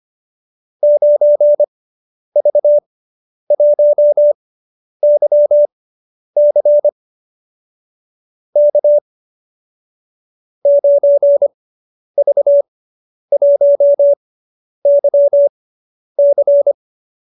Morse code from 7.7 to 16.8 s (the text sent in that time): K 9V1YC